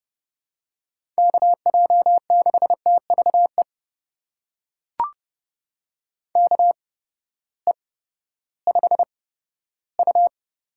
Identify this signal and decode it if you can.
Morse code, heard as a keyed tone, about 30 wpm, spaced as for 15 wpm, 700 Hz. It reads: KJ6TVE KE5U
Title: USA Call Signs - 8x Character Spacing - 30wpm